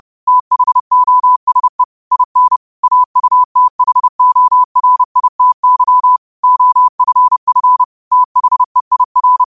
Morse code, heard as a keyed tone, about 30 wpm, 1000 Hz. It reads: THOSE IN AUTHORITY OFF THEIR
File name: qotdfiles / 30wpm